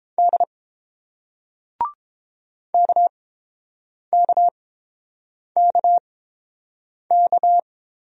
Morse code, heard as a keyed tone, about 30 wpm, spaced as for 7 wpm, 700 Hz. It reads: D KKKK